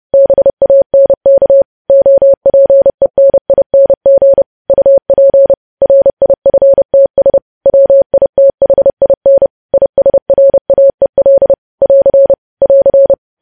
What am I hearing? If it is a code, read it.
Morse code, heard as a keyed tone, about 30 wpm, 550 Hz. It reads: BANK OPENING UP RIFTS WITHIN ISRAEL <AR> <AR>
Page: News Headlines in Morse Code at 30 WPM